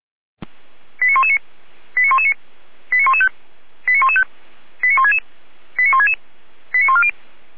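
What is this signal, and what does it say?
Five-tone selective call (ZVEI), heard as quick runs of five tones, tones of about 80 ms. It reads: 89119 89119 89115 89115 89166 89166 89140
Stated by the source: ZVEI five-tone selective call, code 89084, received with an SDR.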